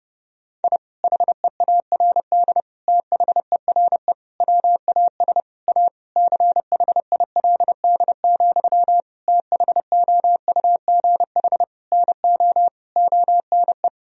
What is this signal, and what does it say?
Morse code, heard as a keyed tone, about 30 wpm, 700 Hz. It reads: I HEARD THERE WAS A CHILD, THOUGH NO ONE
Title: Sentences from Top 1000 Words - Review Part 2 - 30wpm